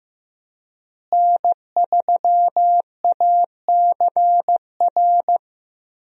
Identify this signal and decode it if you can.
Morse code, heard as a keyed tone, about 15 wpm, 700 Hz. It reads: N3ACR
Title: USA Call Signs - 5 Characters - Repeated 3x - 15wpm